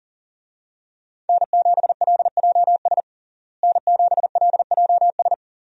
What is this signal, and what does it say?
Morse code, heard as a keyed tone, about 40 wpm, 700 Hz. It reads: N7LJS N7LJS
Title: USA Call Signs - 5 Characters - Repeated 2x - 40wpm